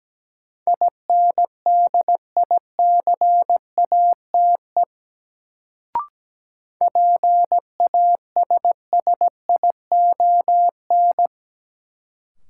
Morse code, heard as a keyed tone, about 17 wpm, 700 Hz. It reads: INDICATE PASSION
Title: Sentences from Top 1300 Words - New Word Drill - 17wpm